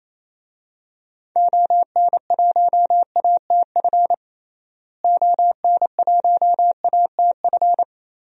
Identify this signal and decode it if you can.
Morse code, heard as a keyed tone, about 28 wpm, 700 Hz. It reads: ON1ATF ON1ATF
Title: DXCC-100 Call Signs - 6 Characters - Repeated 3x - 28wpm